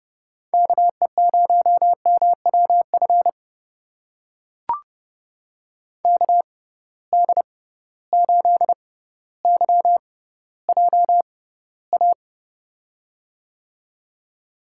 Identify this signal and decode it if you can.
Morse code, heard as a keyed tone, about 32 wpm, spaced as for 16 wpm, 700 Hz. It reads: KE0MWF KD8YJA
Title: USA Call Signs - 6 Characters - 6x Character Spacing - 30wpm